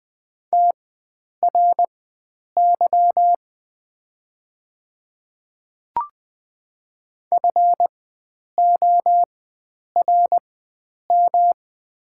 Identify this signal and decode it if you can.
Morse code, heard as a keyed tone, about 20 wpm, spaced as for 9 wpm, 700 Hz. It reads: TRY FORM
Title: 4x Character Spacing - Top 300 Words - 20wpm